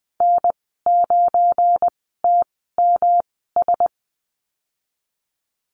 Morse code, heard as a keyed tone, about 20 wpm, spaced as for 14 wpm, 700 Hz. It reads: N9TMS
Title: 238 Lesson 38g - Letter q - Call Signs 20wpm